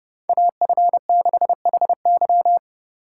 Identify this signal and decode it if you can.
Morse code, heard as a keyed tone, about 30 wpm, 700 Hz. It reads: AF6HY
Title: USA Call Signs - 5 Characters - 3x Character Spacing - 30wpm